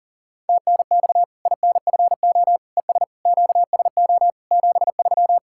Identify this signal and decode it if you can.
Morse code, heard as a keyed tone, about 38 wpm, 700 Hz. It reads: TNX INFO ES QSO 73
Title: Practice QSO - Exchange 4 - Ending - 40wpm